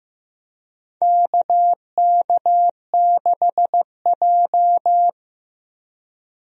Morse code, heard as a keyed tone, about 15 wpm, 700 Hz. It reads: KK6J
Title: USA Call Signs - 4 Characters - 15wpm